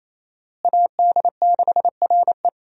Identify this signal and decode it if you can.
Morse code, heard as a keyed tone, about 28 wpm, 700 Hz. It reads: AD6RE
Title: USA Call Signs - 5 Characters - Repeated 3x - 28wpm